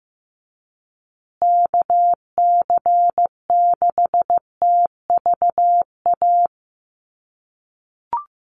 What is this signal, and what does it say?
Morse code, heard as a keyed tone, about 15 wpm, 700 Hz. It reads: KC6TVA
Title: USA Call Signs Repeated 3x 15wpm